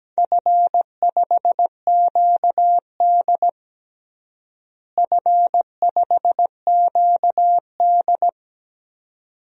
Morse code, heard as a keyed tone, about 17 wpm, 700 Hz. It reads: F5QD F5QD
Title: DXCC-100 Call Signs - 4 Characters - Repeated 2x - 17wpm